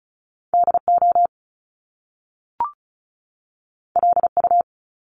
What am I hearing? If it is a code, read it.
Morse code, heard as a keyed tone, about 36 wpm, 700 Hz. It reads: DO LU
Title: Top Two Letter Combinations in Words 35wpm